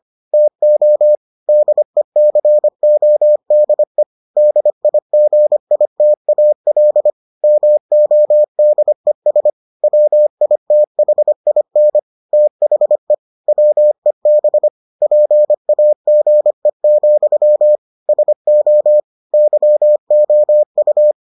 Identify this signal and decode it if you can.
Morse code, heard as a keyed tone, about 25 wpm, 600 Hz. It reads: TO DECODE DIGITAL MODES WITHIN THE WEB PAGE, SO YOU